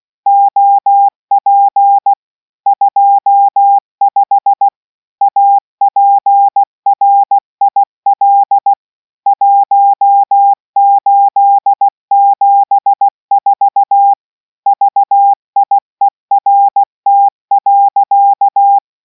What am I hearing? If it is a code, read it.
Morse code, heard as a keyed tone, about 16 wpm, 800 Hz. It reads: OP 25 APRIL 1874 VIERT.